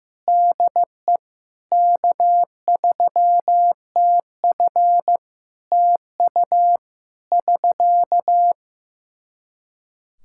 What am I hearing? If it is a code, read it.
Morse code, heard as a keyed tone, about 15 wpm, 700 Hz. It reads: DE K3TF TU <SK>
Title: Practice QSO - Exchange 4 - Ending - 15wpm